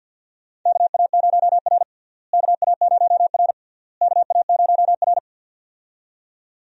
Morse code, heard as a keyed tone, about 52 wpm, 700 Hz. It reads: KA0R KA0R KA0R
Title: USA Call Signs - 4 Characters - Repeated 3x - 50wpm